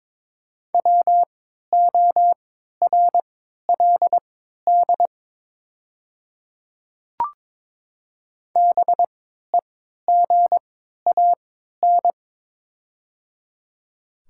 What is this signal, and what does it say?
Morse code, heard as a keyed tone, about 23 wpm, spaced as for 13 wpm, 700 Hz. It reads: WORLD BEGAN
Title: 3x Character Spacing - Top 300 Words - 22wpm